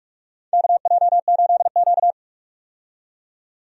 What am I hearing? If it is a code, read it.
Morse code, heard as a keyed tone, about 44 wpm, 700 Hz. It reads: KJ8Q
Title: USA Call Signs - 4 Characters - 45wpm